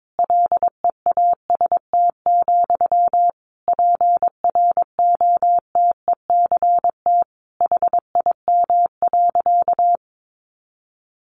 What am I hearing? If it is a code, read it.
Morse code, heard as a keyed tone, about 22 wpm, 700 Hz. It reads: LEAST, PROTECT HIM.
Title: Sentences from Top 900 Words 22wpm